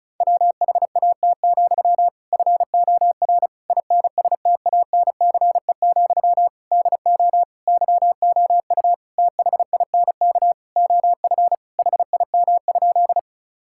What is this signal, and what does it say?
Morse code, heard as a keyed tone, about 34 wpm, 700 Hz. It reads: WHAT, FOR INSTANCE, DO YOU THINK OF HIM?
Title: Sentences from Top 1000 Words - Review Part 5 - 35wpm